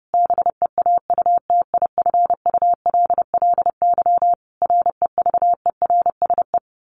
Morse code, heard as a keyed tone, about 30 wpm, 700 Hz. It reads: BEAUTIFULLY REVERSE
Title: A Sound Of Thunder 30wpm